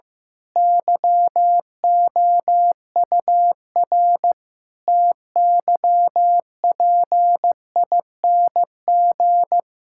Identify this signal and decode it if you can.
Morse code, heard as a keyed tone, about 15 wpm, 700 Hz. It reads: YOUR TYPING
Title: Pangram Challenge v2 - 15wpm